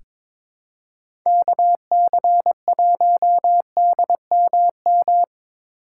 Morse code, heard as a keyed tone, about 22 wpm, 700 Hz. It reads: KC1DMM
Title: USA Call Signs - 6 Characters - 4x Character Spacing - 22wpm